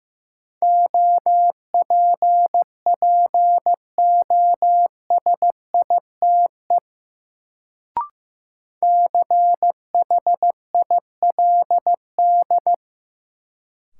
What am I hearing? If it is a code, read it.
Morse code, heard as a keyed tone, about 16 wpm, 700 Hz. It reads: OPPOSITE CHILD
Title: Sets of 5 Words Encore - New Word Drill - 15wpm